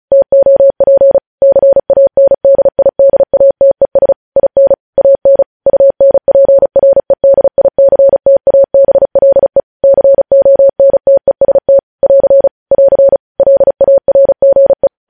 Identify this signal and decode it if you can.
Morse code, heard as a keyed tone, about 36 wpm, 550 Hz. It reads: TOP CANDIDATES IN AN UNPREDICTABLE CONTEST <AR> <AR> LARGE
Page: news headlines in morse code at 35 wpm